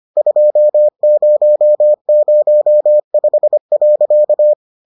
Morse code, heard as a keyed tone, about 25 wpm, 600 Hz. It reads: 2005.